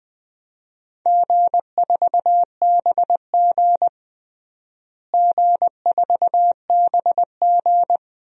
Morse code, heard as a keyed tone, about 20 wpm, 700 Hz. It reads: G4BG G4BG
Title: DXCC-100 Call Signs - 4 Characters - Repeated 2x - 20wpm